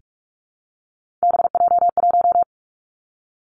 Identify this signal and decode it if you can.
Morse code, heard as a keyed tone, about 46 wpm, 700 Hz. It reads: BJ1
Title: Three Challenging Letter-Number Combinations 45wpm